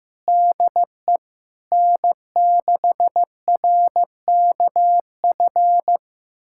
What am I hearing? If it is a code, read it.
Morse code, heard as a keyed tone, about 15 wpm, 700 Hz. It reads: DE N6RKF
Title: USA Callsign DE Callsign - 5 Characters - 15wpm